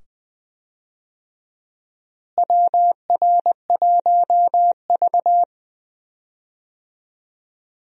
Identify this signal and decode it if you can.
Morse code, heard as a keyed tone, about 20 wpm, 700 Hz. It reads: WR1V
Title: USA Call Signs - 4 Characters - Repeated 2x - 20wpm